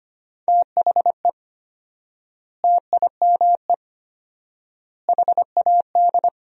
Morse code, heard as a keyed tone, about 25 wpm, 700 Hz. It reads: THE TIME HAD
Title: 4x Word-Spacing - Sentences from Top 100 Words - 25wpm